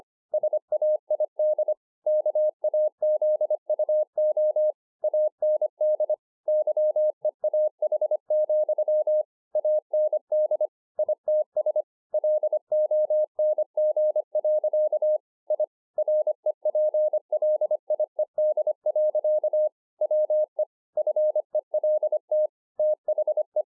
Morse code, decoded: SAID KAZUO AND YEAH, AND ITS LONG. I REPLIED. WE FELT THE

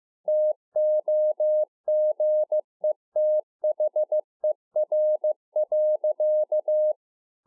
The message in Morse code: TOGETHER.

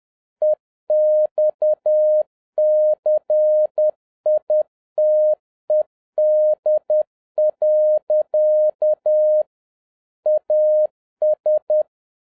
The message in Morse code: EXCITED. AS

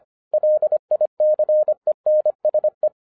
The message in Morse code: LICENSE

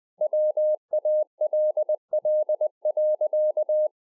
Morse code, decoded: WALL.

